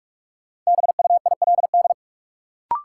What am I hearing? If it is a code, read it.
Morse code, heard as a keyed tone, about 46 wpm, 700 Hz. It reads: BUILD
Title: Sentences from Top 1100 Words - New Word Drill Alt - 45wpm